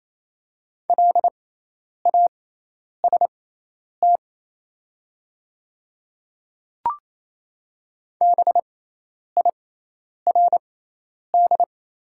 Morse code, heard as a keyed tone, about 28 wpm, spaced as for 10 wpm, 700 Hz. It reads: LAST BIRD
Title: 6x Character Spacing - Top 500 Words - 28wpm